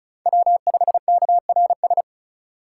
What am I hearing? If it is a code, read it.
Morse code, heard as a keyed tone, about 36 wpm, 700 Hz. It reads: W5KRS
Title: USA Call Signs - 4x Character Spacing - 35wpm